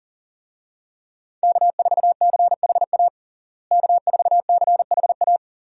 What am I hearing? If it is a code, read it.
Morse code, heard as a keyed tone, about 40 wpm, 700 Hz. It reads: K4CHA K4CHA
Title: USA Call Signs - 5 Characters - Repeated 2x - 40wpm